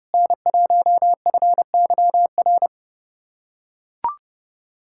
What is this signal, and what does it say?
Morse code, heard as a keyed tone, about 30 wpm, 700 Hz. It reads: N1FYR
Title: USA Callsign DE Callsign 30wpm